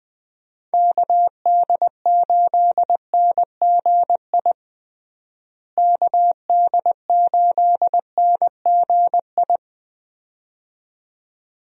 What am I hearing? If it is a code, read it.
Morse code, heard as a keyed tone, about 20 wpm, 700 Hz. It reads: KD8NGI KD8NGI
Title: USA Call Signs - 6 Characters - Repeated 2x - 20wpm